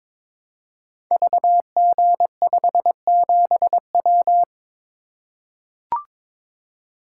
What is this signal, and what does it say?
Morse code, heard as a keyed tone, about 23 wpm, 700 Hz. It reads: VG57W
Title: Code Groups - 5 Letter:Numbers - 22wpm